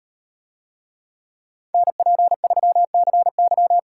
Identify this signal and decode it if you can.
Morse code, heard as a keyed tone, about 38 wpm, 700 Hz. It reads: NP3CY